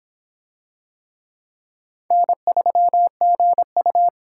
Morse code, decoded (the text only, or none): N3GU